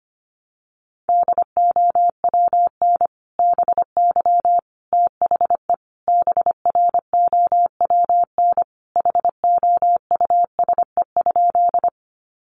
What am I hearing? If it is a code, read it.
Morse code, heard as a keyed tone, about 25 wpm, 700 Hz. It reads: DOWN BY THE BROWN HOUSE?
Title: Sentences from Top 1200 Words 25wpm